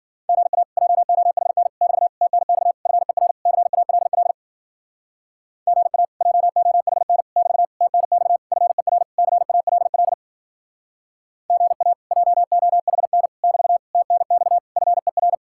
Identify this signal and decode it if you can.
Morse code, heard as a keyed tone, about 52 wpm, 700 Hz. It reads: GA JOHN <BT> TNX FER CALL GA JOHN <BT> TNX FER CALL GA JOHN <BT> TNX FER